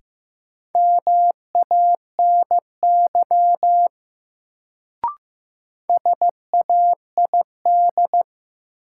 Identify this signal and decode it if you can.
Morse code, heard as a keyed tone, about 15 wpm, 700 Hz. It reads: MANY SAID
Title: Sentences from Top 100 Words - New Word Drill - 15wpm